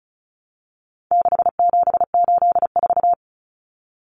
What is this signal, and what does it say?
Morse code, heard as a keyed tone, about 34 wpm, 700 Hz. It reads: 6784
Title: Common Cut Numbers - 4-Digit 35wpm